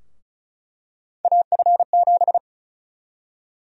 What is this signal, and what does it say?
Morse code, heard as a keyed tone, about 36 wpm, 700 Hz. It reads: AF7
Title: USA Call Sign Prefixes - Repeated 2x - 35wpm